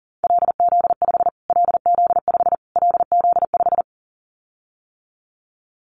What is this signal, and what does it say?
Morse code, heard as a keyed tone, about 40 wpm, 700 Hz. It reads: LZ5 LZ5 LZ5
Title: Callsign Prefixes for Top 25 DXCC - Repeated 3x - 40wpm